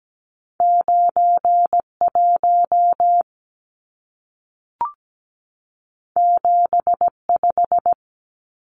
Morse code, heard as a keyed tone, about 17 wpm, 700 Hz. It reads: 91 75
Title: Numbers - 2-Digit 17wpm